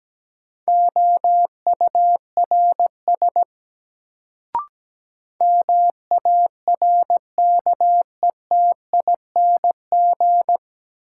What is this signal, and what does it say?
Morse code, heard as a keyed tone, about 17 wpm, 700 Hz. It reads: OURS MARKETING
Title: Sentences from Top 1000 Words - Review Part 5 - New Word Drill - 17wpm